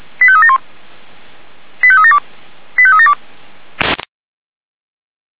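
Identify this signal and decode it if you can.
Five-tone selective call (ZVEI), heard as quick runs of five tones, tones of about 70 ms. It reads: 85371 85372 85372
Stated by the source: ZVEI five-tone selective call received on SDR.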